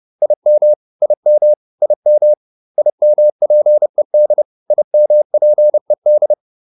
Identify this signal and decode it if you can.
Morse code, heard as a keyed tone, about 30 wpm, 600 Hz. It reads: IM IM IM IMPED IMPED
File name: IM5 30